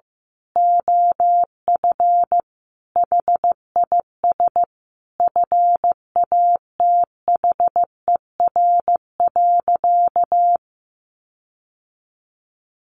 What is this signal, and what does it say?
Morse code, heard as a keyed tone, about 15 wpm, 700 Hz. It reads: OF HIS FATHER.
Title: Sentences from Top 1500 Words 15wpm